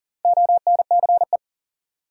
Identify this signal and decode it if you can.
Morse code, heard as a keyed tone, about 40 wpm, 700 Hz. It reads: ONCE
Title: Sentences from Top 300 Words - New Word Drill - 40wpm